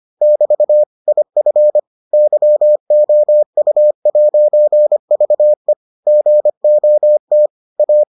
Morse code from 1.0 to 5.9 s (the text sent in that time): IF YOU'VE